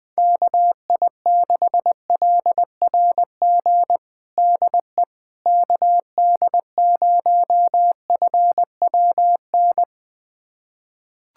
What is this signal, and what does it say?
Morse code, heard as a keyed tone, about 20 wpm, 700 Hz. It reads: KI6LRG DE KD0FWN
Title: USA Callsign DE Callsign - 6 Characters - 20wpm